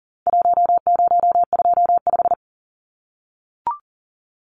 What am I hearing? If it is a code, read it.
Morse code, heard as a keyed tone, about 40 wpm, 700 Hz. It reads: 1025
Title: Numbers - Contest Style - 4-Digit 40wpm